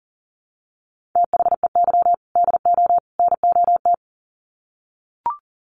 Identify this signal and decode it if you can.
Morse code, heard as a keyed tone, about 40 wpm, 700 Hz. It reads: THEY DO NOT
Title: Sets of 3 Words 40wpm